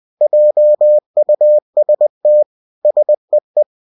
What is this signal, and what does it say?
Morse code, heard as a keyed tone, about 20 wpm, 600 Hz. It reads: JUST SEE
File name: cw-books / winnie-the-pooh / winnie-20wpm-page-0042